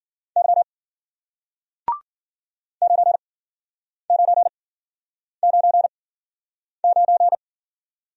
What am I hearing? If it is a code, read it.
Morse code, heard as a keyed tone, about 54 wpm, spaced as for 9 wpm, 700 Hz. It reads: Y 9999